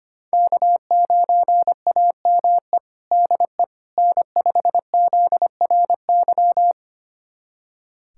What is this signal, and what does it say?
Morse code, heard as a keyed tone, about 25 wpm, 700 Hz. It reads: K9AME DE N5ZRY
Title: USA Callsign DE Callsign - 5 Characters - 25wpm